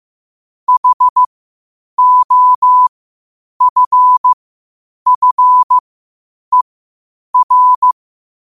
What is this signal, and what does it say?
Morse code, heard as a keyed tone, about 15 wpm, spaced as for 8 wpm, 1000 Hz. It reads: HOFFER
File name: qotdfiles / 8wpm